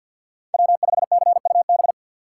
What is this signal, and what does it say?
Morse code, heard as a keyed tone, about 53 wpm, 690 Hz. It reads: W5CUB
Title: USA Call Signs - 5 Characters - 6x Character Spacing - 50wpm